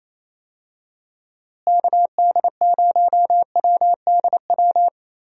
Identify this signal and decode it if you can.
Morse code, heard as a keyed tone, about 28 wpm, 700 Hz. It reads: KD0WDW
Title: USA Call Signs - 6 Characters - Repeated 3x - 28wpm